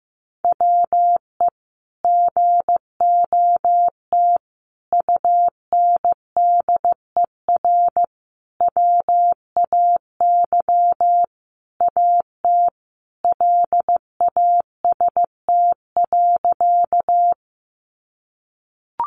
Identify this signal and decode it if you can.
Morse code, heard as a keyed tone, about 15 wpm, 700 Hz. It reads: WE GOT UNDER WAY AT LAST.
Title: Sentences from Top 300 Words 15wpm